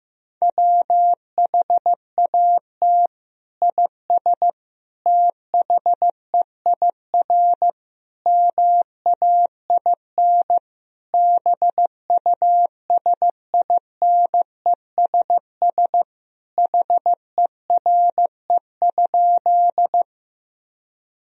Morse code, heard as a keyed tone, about 15 wpm, 700 Hz. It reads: WHAT IS THEIR MAIN BUSINESS HERE?
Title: Sentences from Top 1000 Words - Review Part 2 - 15wpm